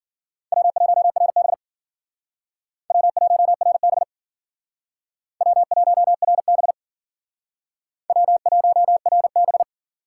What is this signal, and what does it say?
Morse code, heard as a keyed tone, about 51 wpm, 700 Hz. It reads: W1RB W1RB W1RB W1RB